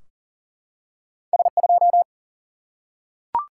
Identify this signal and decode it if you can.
Morse code, heard as a keyed tone, about 43 wpm, 700 Hz. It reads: S 2
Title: Single-Letter Single-Number 40wpm